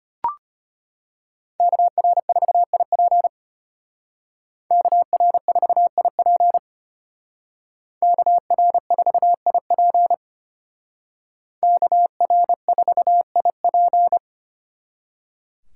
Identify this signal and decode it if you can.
Morse code, heard as a keyed tone, about 33 wpm, 700 Hz. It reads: KR4IP KR4IP KR4IP KR4IP